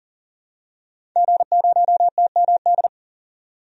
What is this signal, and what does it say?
Morse code, heard as a keyed tone, about 40 wpm, 700 Hz. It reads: G0TMD
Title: DXCC-100 Call Signs - 5 Characters - 40wpm